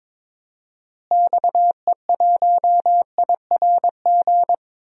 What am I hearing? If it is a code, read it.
Morse code, heard as a keyed tone, about 21 wpm, 700 Hz. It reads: XE1IRG
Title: DXCC-100 Call Signs - 6 Characters - Repeated 2x - 22wpm